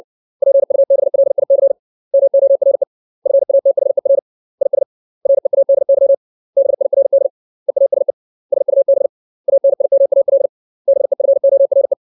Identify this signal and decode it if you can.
Morse code, heard as a keyed tone, about 60 wpm, 550 Hz. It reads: PUBLIC MORE FATHER IS READY BIRD EASE SUB ANIMAL BEFORE
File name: 632 words / 632 Words 60WPM